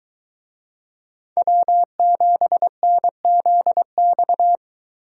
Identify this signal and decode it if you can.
Morse code, heard as a keyed tone, about 23 wpm, 700 Hz. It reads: W7NZX